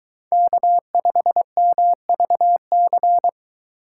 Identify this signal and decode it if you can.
Morse code, heard as a keyed tone, about 22 wpm, 700 Hz. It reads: K5MVC